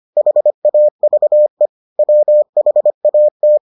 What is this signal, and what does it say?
Morse code, heard as a keyed tone, about 25 wpm, 600 Hz. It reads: HAVE WHAT